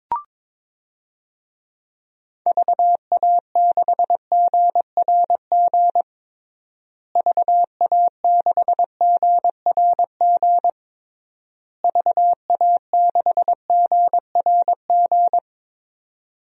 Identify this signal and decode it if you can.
Morse code, heard as a keyed tone, about 21 wpm, 700 Hz. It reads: VA6GRG VA6GRG VA6GRG
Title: DXCC-100 Call Signs - 6 Characters - Repeated 3x - 22wpm